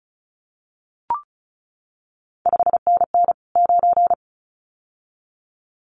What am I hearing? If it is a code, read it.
Morse code, heard as a keyed tone, about 36 wpm, 700 Hz. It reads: 5NN 9
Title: CQ WW DX Contest Exchange - 35wpm